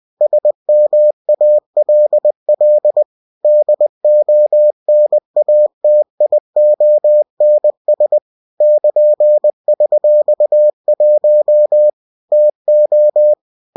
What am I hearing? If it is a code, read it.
Morse code, heard as a keyed tone, about 20 wpm, 600 Hz. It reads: SMALL DONATIONS <KN>$1 TO